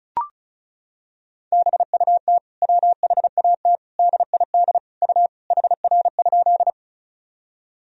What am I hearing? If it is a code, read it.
Morse code, heard as a keyed tone, about 36 wpm, 700 Hz. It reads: BUT WHAT DID U HR?
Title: Sentences from Top 300 Words and Abbreviations 35wpm